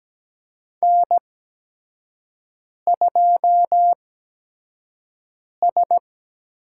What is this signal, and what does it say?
Morse code, heard as a keyed tone, about 17 wpm, spaced as for 5 wpm, 700 Hz. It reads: N2S
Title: USA Call Signs - 5 Characters - 8x Character Spacing - 17wpm